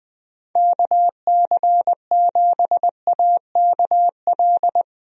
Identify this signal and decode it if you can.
Morse code, heard as a keyed tone, about 21 wpm, 700 Hz. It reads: KC7AKL
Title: USA Call Signs - 6 Characters - Repeated 2x - 20wpm